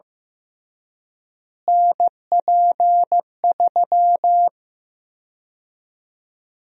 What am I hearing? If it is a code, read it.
Morse code, heard as a keyed tone, about 15 wpm, 700 Hz. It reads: NP3
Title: USA Call Sign Prefixes - Repeated 2x - 15wpm